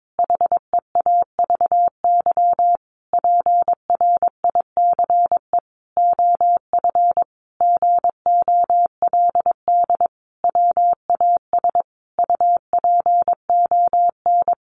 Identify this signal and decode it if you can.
Morse code, heard as a keyed tone, about 22 wpm, 700 Hz. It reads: HEAVY PRICE OF GOLD WAS UPON
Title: Sentences from Top 1100 Words 22wpm